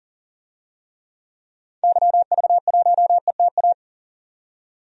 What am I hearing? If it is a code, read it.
Morse code, heard as a keyed tone, about 40 wpm, 700 Hz. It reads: YV1ETA
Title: DXCC-100 Call Signs - 6 Characters - 40wpm